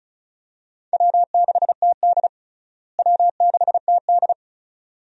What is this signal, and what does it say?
Morse code, heard as a keyed tone, about 34 wpm, 700 Hz. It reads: W6TD W6TD
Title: USA Call Signs - 4 Characters - Repeated 2x - 35wpm